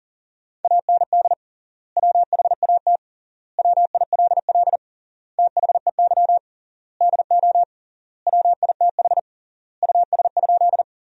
Morse code, heard as a keyed tone, about 40 wpm, 700 Hz. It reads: AND WHAT WILL THEY DO WITH US?
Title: 3x Word-Spacing - Sentences from Top 200 Words - 40wpm